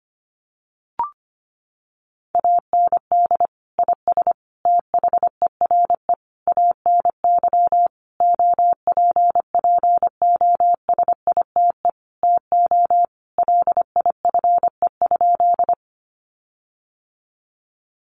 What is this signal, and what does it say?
Morse code, heard as a keyed tone, about 25 wpm, 700 Hz. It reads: AND IS THERE ANY OPPOSITE TO LIFE?
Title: Sentences from Top 1300 Words 25wpm